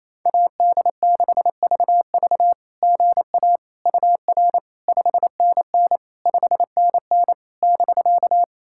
Morse code, heard as a keyed tone, about 28 wpm, 690 Hz. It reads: AD6VV GA UR 5NN 5NN <BK>
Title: SOTA QSO Practice 28wpm